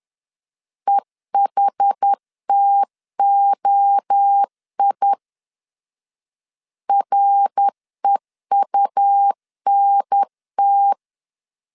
Morse code, heard as a keyed tone, about 10 wpm, 800 Hz. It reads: EHTOI REUNT
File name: CW / DET / 1 3 1500C S C 10wpm 3335